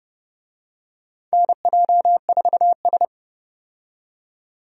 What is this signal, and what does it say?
Morse code, heard as a keyed tone, about 32 wpm, 700 Hz. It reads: NJ4S
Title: USA Call Signs - 4 Characters - 30wpm